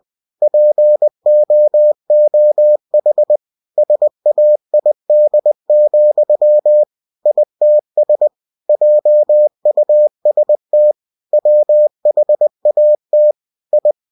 Morse code, heard as a keyed tone, about 20 wpm, 600 Hz. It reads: POOH SAID, ITS JUST WHAT I
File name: cw-books / winnie-the-pooh / winnie-20wpm-page-0034